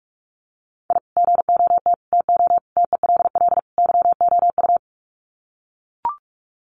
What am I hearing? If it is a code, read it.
Morse code, heard as a keyed tone, about 44 wpm, 700 Hz. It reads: I GOT TO TELL YOU
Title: Sets of 5 Words Encore2 45wpm